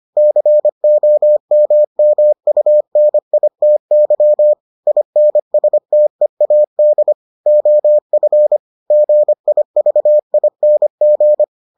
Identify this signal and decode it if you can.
Morse code, heard as a keyed tone, about 25 wpm, 600 Hz. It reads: COMMUNITY INSTEAD OF GIVING